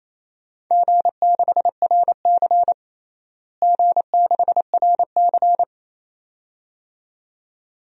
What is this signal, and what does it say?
Morse code, heard as a keyed tone, about 28 wpm, 700 Hz. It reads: G6RC G6RC
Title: DXCC-100 Call Signs - 4 Characters - Repeated 2x - 28wpm